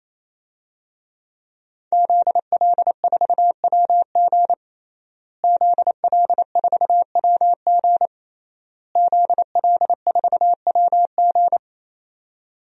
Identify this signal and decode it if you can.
Morse code, heard as a keyed tone, about 28 wpm, 700 Hz. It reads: ZL4WG ZL4WG ZL4WG
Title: DXCC-100 Call Signs - 5 Characters - Repeated 3x - 28wpm